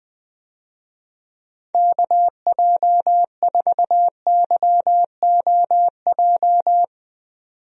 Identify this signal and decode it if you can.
Morse code, heard as a keyed tone, about 20 wpm, 700 Hz. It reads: KJ4YOJ
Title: USA Call Signs - 6 Characters - 20wpm